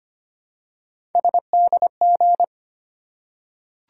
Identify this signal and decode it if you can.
Morse code, heard as a keyed tone, about 25 wpm, 700 Hz. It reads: SDG
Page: morse-code-ninja-practice-25wpm